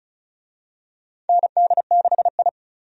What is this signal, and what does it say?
Morse code, heard as a keyed tone, about 36 wpm, 700 Hz. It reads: ND6I
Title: USA Call Signs - 4 Characters - 35wpm